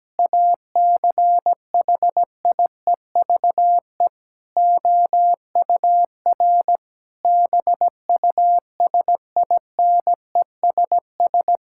Morse code, heard as a keyed tone, about 17 wpm, 700 Hz. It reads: ACHIEVE OUR BUSINESS
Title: Sentences from Top 1000 Words - Review Part 6 - 17wpm